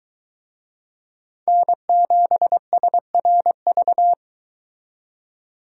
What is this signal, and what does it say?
Morse code, heard as a keyed tone, about 26 wpm, 700 Hz. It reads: N7SRV